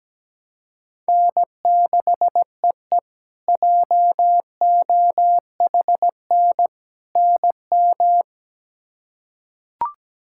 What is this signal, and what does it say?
Morse code, heard as a keyed tone, about 17 wpm, 700 Hz. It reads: N6EE JOHN NM
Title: K1USN SST QSO Practice - 17wpm